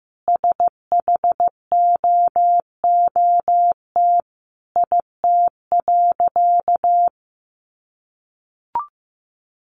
Morse code, heard as a keyed tone, about 15 wpm, 700 Hz. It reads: SHOOT IT.
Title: A Sound Of Thunder 15wpm